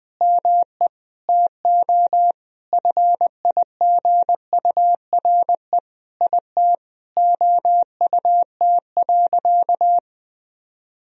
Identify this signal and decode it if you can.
Morse code, heard as a keyed tone, about 21 wpm, 700 Hz. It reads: ME TO FIGURE IT OUT.
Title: Sentences from Top 1000 Words - Review Part 2 - 20wpm